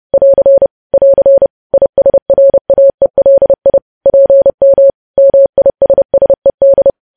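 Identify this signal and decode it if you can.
Morse code, heard as a keyed tone, about 30 wpm, 550 Hz. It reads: <AR> <AR> ISRAELI PM MISSED
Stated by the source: News headlines sent in Morse at 30.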